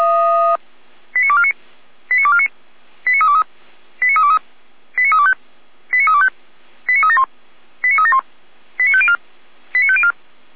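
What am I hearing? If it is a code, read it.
Five-tone selective call (ZVEI), heard as quick runs of five tones, tones of about 70 ms. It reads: 89249 89249 89323 89323 89325 89325 89371 89371 89504 89504
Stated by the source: Five-tone ZVEI call 89250 received on an SDR, from feuerwehr-orlamuende.org (Feuerwehr Orlamünde fire brigade).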